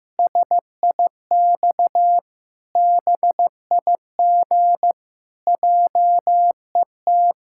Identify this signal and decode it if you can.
Morse code, heard as a keyed tone, about 15 wpm, 700 Hz. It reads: SIX BIG JET
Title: Pangram Challenge v2 - 15wpm